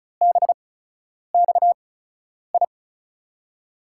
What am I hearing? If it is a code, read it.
Morse code, heard as a keyed tone, about 36 wpm, spaced as for 10 wpm, 700 Hz. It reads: BXI
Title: USA Call Signs - 6 Characters - 8x Character Spacing - 35wpm